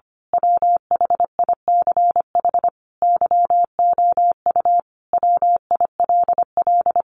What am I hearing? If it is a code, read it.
Morse code, heard as a keyed tone, about 25 wpm, 700 Hz. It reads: WHICH YOU WILL
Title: Sentences from Top 1400 Words 25wpm